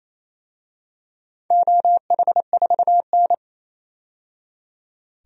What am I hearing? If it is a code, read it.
Morse code, heard as a keyed tone, about 28 wpm, 700 Hz. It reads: OH4N
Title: DXCC-100 Call Signs - 4 Characters - 28wpm